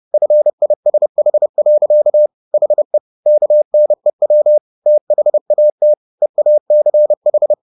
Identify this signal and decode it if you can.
Morse code, heard as a keyed tone, about 30 wpm, 600 Hz. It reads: FISH. HE KNEW THAT EACH